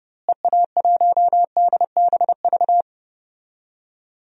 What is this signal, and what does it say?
Morse code, heard as a keyed tone, about 30 wpm, 700 Hz. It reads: EA1DBV
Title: DXCC-100 Call Signs - 6 Characters - Repeated 3x - 30wpm